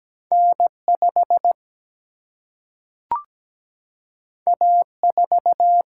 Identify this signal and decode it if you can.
Morse code, heard as a keyed tone, about 17 wpm, 700 Hz. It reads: N5 A4
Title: Single-Letter Single-Number 17wpm